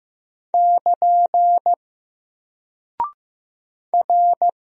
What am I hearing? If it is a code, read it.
Morse code, heard as a keyed tone, about 15 wpm, 700 Hz. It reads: <KN> R
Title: Single Character (Letter:Number:Punctuation:Prosign) Alt 15wpm